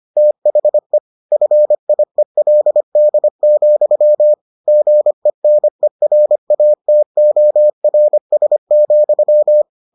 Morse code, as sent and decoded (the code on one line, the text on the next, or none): - .... / ..-. .. .-.. -.. --..-- / --. -. .-. .- - --- .-. ... --..--
THE FIELD, GENERATORS,